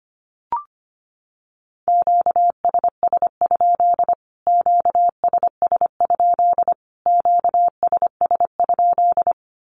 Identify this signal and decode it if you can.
Morse code, heard as a keyed tone, about 25 wpm, 700 Hz. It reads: QSS? QSS? QSS?
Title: Q-Codes Repeated 3x 25wpm